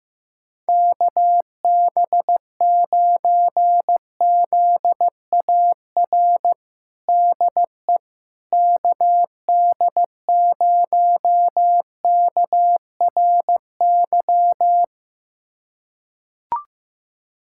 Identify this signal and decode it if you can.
Morse code, heard as a keyed tone, about 15 wpm, 700 Hz. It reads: KB9ZAR DE KD0KRY
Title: USA Callsign DE Callsign - 6 Characters - 15wpm